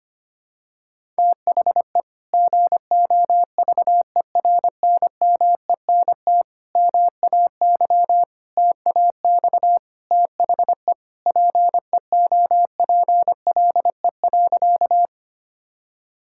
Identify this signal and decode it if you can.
Morse code, heard as a keyed tone, about 25 wpm, 700 Hz. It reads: THE GOVERNMENT MAY TAX THE PEOPLE.
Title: Sentences from Top 1000 Words - Review Part 5 - 25wpm